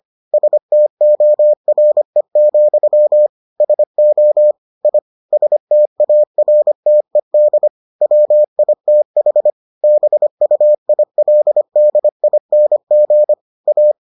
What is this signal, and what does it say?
Morse code, heard as a keyed tone, about 25 wpm, 600 Hz. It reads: STORE, SO I STARTED WITH BUILDING A